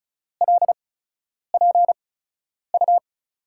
Morse code, heard as a keyed tone, about 36 wpm, spaced as for 10 wpm, 700 Hz. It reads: LPU
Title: USA Call Signs - 5 Characters - 8x Character Spacing - 35wpm